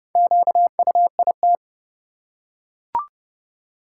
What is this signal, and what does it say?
Morse code, heard as a keyed tone, about 30 wpm, 700 Hz. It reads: QUIT
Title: Four Letter Words - 30wpm